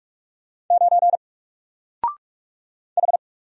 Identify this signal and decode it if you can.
Morse code, heard as a keyed tone, about 44 wpm, 700 Hz. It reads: 9 H
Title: Single Character (Letter:Number:Punctuation:Prosign) Alt 45wpm